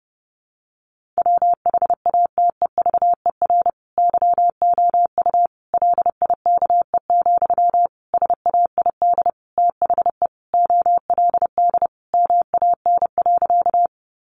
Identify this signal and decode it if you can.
Morse code, heard as a keyed tone, about 30 wpm, 700 Hz. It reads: WHATEVER YOU LIKE, SAID THE OLD MAN.
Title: Sentences from Top 1100 Words 30wpm